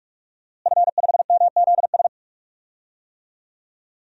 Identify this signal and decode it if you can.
Morse code, heard as a keyed tone, about 42 wpm, 700 Hz. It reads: F5MZS
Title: DXCC-100 Call Signs - 5 Characters - Repeated 3x - 45wpm